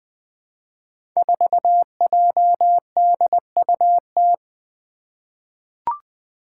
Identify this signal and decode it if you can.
Morse code, heard as a keyed tone, about 20 wpm, 700 Hz. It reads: 4JDUT
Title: Five Challenging Letter-Number Combinations 20wpm